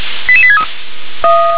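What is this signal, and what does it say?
Five-tone selective call (ZVEI), heard as quick runs of five tones, tones of about 70 ms. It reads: 88863
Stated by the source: ZVEI five-tone selective call received on an SDR.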